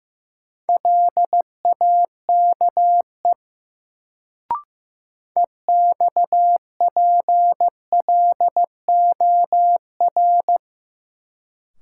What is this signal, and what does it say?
Morse code, heard as a keyed tone, about 15 wpm, 700 Hz. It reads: LAKE EXPLOR
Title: Sentences from Top 1100 Words - New Word Drill Alt - 15wpm